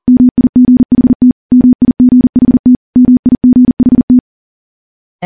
Morse code, heard as a keyed tone, about 40 wpm, 250 Hz. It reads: MIGHT MIGHT MIGHT
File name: morse / morse-training-40WPM